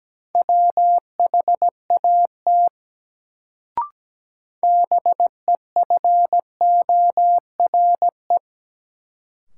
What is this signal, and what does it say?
Morse code, heard as a keyed tone, about 17 wpm, 700 Hz. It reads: WHAT BEFORE
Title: Sets of 2 Words - New Word Drill - 17wpm